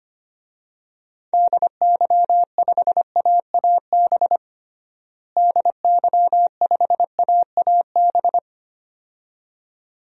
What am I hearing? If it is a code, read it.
Morse code, heard as a keyed tone, about 25 wpm, 700 Hz. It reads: DY5AAB DY5AAB
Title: DXCC-100 Call Signs - 6 Characters - Repeated 2x - 25wpm